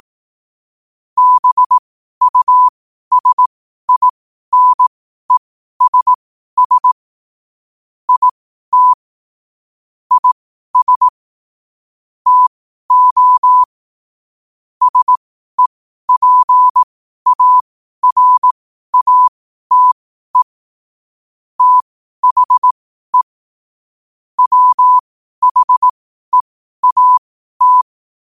Morse code, heard as a keyed tone, about 18 wpm, spaced as for 12 wpm, 1000 Hz. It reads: BUSINESS IT IS TO SEPARATE THE WHEAT
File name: qotdfiles / 12wpm